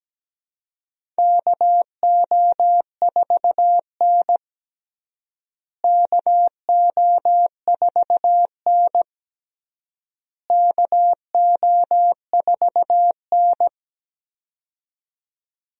Morse code, heard as a keyed tone, about 17 wpm, 700 Hz. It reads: KO4N KO4N KO4N
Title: USA Call Signs - 4 Characters - Repeated 3x - 17wpm